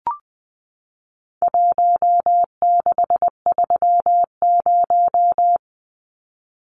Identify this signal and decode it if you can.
Morse code, heard as a keyed tone, about 20 wpm, 700 Hz. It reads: 1630
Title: Numbers - Contest Style - 4-Digit 20wpm